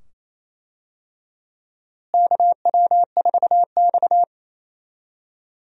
Morse code, heard as a keyed tone, about 28 wpm, 700 Hz. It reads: KW4X
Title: USA Call Signs - 4 Characters - Repeated 2x - 28wpm